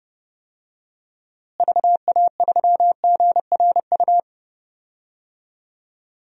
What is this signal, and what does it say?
Morse code, heard as a keyed tone, about 30 wpm, 700 Hz. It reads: VA3GRU
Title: DXCC-100 Call Signs - 6 Characters - Repeated 2x - 30wpm